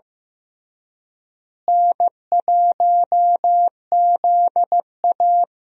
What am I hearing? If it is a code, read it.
Morse code, heard as a keyed tone, about 16 wpm, 700 Hz. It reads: N1ZA
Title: USA Call Signs - 4 Characters - Repeated 2x - 15wpm